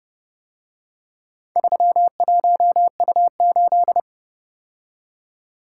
Morse code, heard as a keyed tone, about 32 wpm, 700 Hz. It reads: 31U8